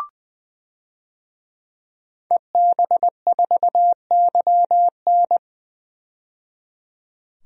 Morse code, heard as a keyed tone, about 20 wpm, 700 Hz. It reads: EB4YN
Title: DXCC-100 Call Signs - 5 Characters - 20wpm